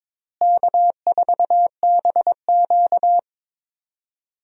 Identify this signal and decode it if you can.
Morse code, heard as a keyed tone, about 21 wpm, 700 Hz. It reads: K4BQ
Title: USA Call Signs - 4 Characters - Repeated 2x - 22wpm